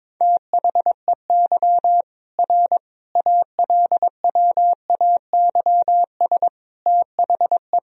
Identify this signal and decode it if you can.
Morse code, heard as a keyed tone, about 22 wpm, 700 Hz. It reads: THEY R ALWAYS THE
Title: Sentences from Top 300 Words and Abbreviations 22wpm